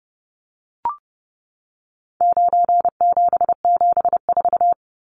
Morse code, heard as a keyed tone, about 29 wpm, 700 Hz. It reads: 9774
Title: Numbers - 4-Digit 30wpm